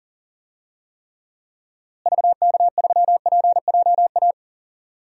Morse code, heard as a keyed tone, about 40 wpm, 700 Hz. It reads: VK3PJA
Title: DXCC-100 Call Signs - 6 Characters - Repeated 2x - 40wpm